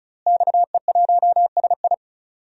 Morse code, heard as a keyed tone, about 36 wpm, 700 Hz. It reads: XE1SI
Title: DXCC-100 Call Signs - 5 Characters - Repeated 3x - 35wpm